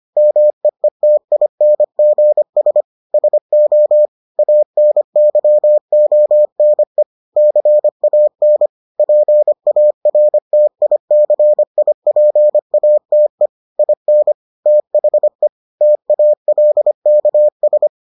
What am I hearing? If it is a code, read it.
Morse code, heard as a keyed tone, about 25 wpm, 600 Hz. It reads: MEETINGS SO ANYONE CAN PARTICIPATE IN THE TALKS